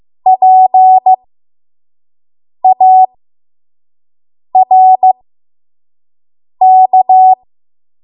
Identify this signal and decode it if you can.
Morse code, heard as a keyed tone, about 15 wpm, spaced as for 5 wpm, 750 Hz. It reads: PARK
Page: 5 WPM Code Archive